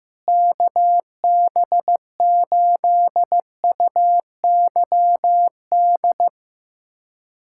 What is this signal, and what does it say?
Morse code, heard as a keyed tone, about 15 wpm, 700 Hz. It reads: KB8UYD
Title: USA Callsign DE Callsign 15wpm